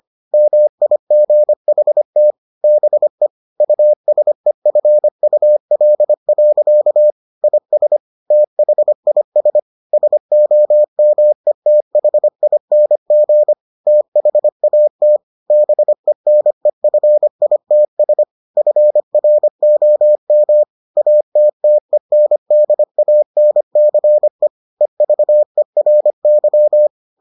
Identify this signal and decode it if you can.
Morse code, heard as a keyed tone, about 25 wpm, 600 Hz. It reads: MIGHT BE USEFUL. IS THIS SOMETHING THAT BENEFITS FROM ATTENDANCE EVERY